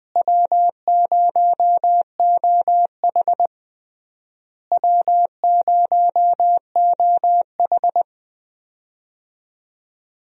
Morse code, heard as a keyed tone, about 20 wpm, 700 Hz. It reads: W0OH W0OH
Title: USA Call Signs - 4 Characters - Repeated 3x - 20wpm